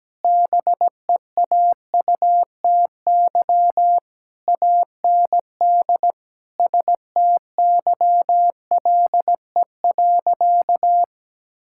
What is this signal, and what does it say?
Morse code, heard as a keyed tone, about 17 wpm, 700 Hz. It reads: BEAUTY AND STYLE.
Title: Sentences from Top 1000 Words - Review Part 5 - 17wpm